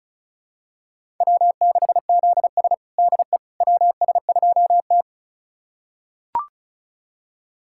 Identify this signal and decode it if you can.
Morse code, heard as a keyed tone, about 36 wpm, 700 Hz. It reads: W6ZS DE WS2T
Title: USA Callsign DE Callsign - 4 Characters - 35wpm